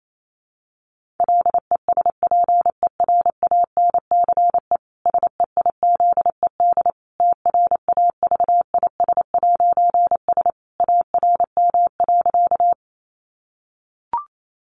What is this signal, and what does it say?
Morse code, heard as a keyed tone, about 28 wpm, 700 Hz. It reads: LESPERANCE SEIZED TRAVIS'S ARM.
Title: A Sound Of Thunder 28wpm